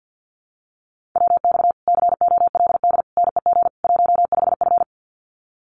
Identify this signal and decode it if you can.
Morse code, heard as a keyed tone, about 50 wpm, 700 Hz. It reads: WX COLD NEG 15F
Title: Common QSO Phrases Encore 50wpm